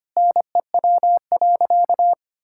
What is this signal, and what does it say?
Morse code, heard as a keyed tone, about 25 wpm, 700 Hz. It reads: NEW.
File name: practice / 25wpm / Sentences with 9-Word Phrases - 25wpm